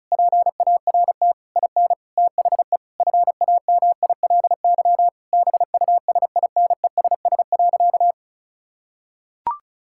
Morse code, heard as a keyed tone, about 36 wpm, 700 Hz. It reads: PART IN THE FAMILY BUSINESS.
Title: Sentences from Top 1000 Words - Review Part 6 - 35wpm